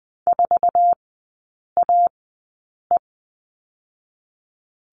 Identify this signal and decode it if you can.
Morse code, heard as a keyed tone, about 20 wpm, spaced as for 8 wpm, 700 Hz. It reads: 4AE
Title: 060 Lesson 11e - Letter h - Three Characters 20wpm